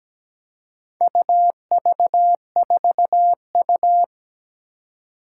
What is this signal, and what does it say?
Morse code, heard as a keyed tone, about 17 wpm, 700 Hz. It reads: UV4U
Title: Four Challenging Letter-Number Combinations 17wpm